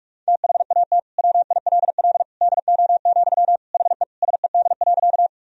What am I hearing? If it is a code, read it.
Morse code, heard as a keyed tone, about 46 wpm, 700 Hz. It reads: THAT WILL DO, HE SED.
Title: Sentences from Top 100 Words and Abbreviations 45wpm